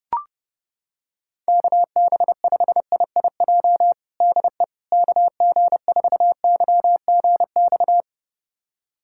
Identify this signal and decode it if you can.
Morse code, heard as a keyed tone, about 30 wpm, 700 Hz. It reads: KB5IIJ DE KG4YGX
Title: USA Callsign DE Callsign - 6 Characters - 30wpm